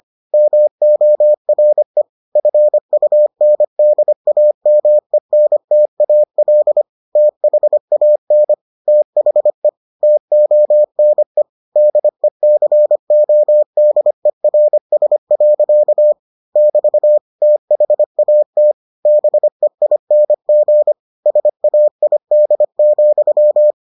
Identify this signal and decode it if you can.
Morse code, heard as a keyed tone, about 25 wpm, 600 Hz. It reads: MORE FUNDAMENTAL THAN THE TONE DECODERS. <BT> THAT BEING SAID,